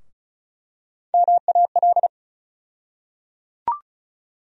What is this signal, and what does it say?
Morse code, heard as a keyed tone, about 36 wpm, 700 Hz. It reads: MAL
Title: USA Call Sign Postfixes - 3 Letters 35wpm